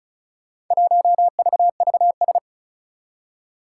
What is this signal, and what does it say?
Morse code, heard as a keyed tone, about 36 wpm, 700 Hz. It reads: 1VVS
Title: Four Challenging Letter-Number Combinations 35wpm